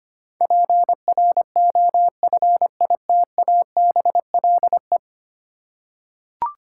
Morse code, heard as a keyed tone, about 25 wpm, 700 Hz. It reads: PROFITABLE
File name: Ten Letter Words - 25wpm